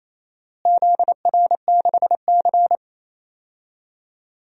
Morse code, heard as a keyed tone, about 28 wpm, 700 Hz. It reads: ZR6C
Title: DXCC-100 Call Signs - 4 Characters - Repeated 3x - 28wpm